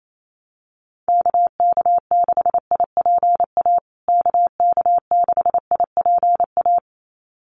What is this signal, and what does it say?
Morse code, heard as a keyed tone, about 28 wpm, 700 Hz. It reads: KK6IPA KK6IPA
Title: USA Call Signs Repeated 2x 28wpm